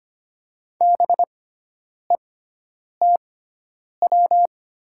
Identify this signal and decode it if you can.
Morse code, heard as a keyed tone, about 25 wpm, spaced as for 9 wpm, 700 Hz. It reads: BETW
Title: 6x Character Spacing - Top 300 Words - 25wpm